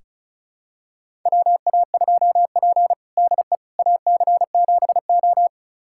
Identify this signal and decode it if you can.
Morse code, heard as a keyed tone, about 36 wpm, 700 Hz. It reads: WA2P DE AC7O
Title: USA Callsign DE Callsign - 4 Characters - 35wpm